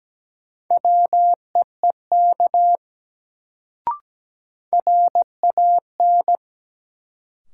Morse code, heard as a keyed tone, about 17 wpm, 700 Hz. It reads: WEEK RAN
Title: Sentences from Top 500 Words - New Word Drill - 17wpm